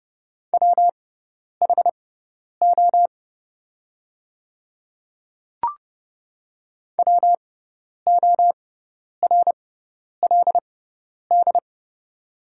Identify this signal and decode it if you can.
Morse code, heard as a keyed tone, about 30 wpm, spaced as for 10 wpm, 700 Hz. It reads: WHO WORLD
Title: 6x Character Spacing - Top 200 Words - 30wpm